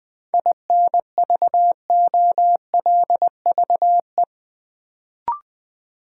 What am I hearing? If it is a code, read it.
Morse code, heard as a keyed tone, about 20 wpm, 700 Hz. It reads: INVOLVE
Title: Sentences from Top 1200 Words - New Word Drill - 20wpm